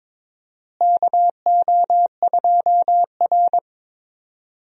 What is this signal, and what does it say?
Morse code, heard as a keyed tone, about 22 wpm, 700 Hz. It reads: KO2R
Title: USA Call Signs - 4 Characters - Repeated 3x - 22wpm